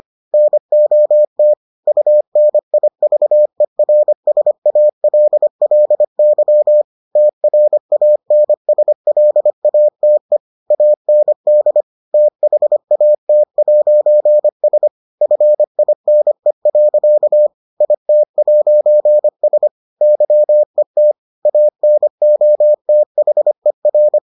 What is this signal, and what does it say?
Morse code, heard as a keyed tone, about 25 wpm, 600 Hz. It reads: NOT UNIVERSALLY TRANSLATE AND THAT'S FINE. IT'S YET ANOTHER